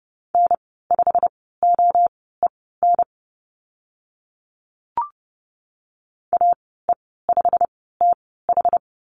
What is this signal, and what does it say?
Morse code, heard as a keyed tone, about 30 wpm, spaced as for 17 wpm, 700 Hz. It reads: N5OEN AE5TH
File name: Lesson 15g - Number 5 - Call Signs 30wpm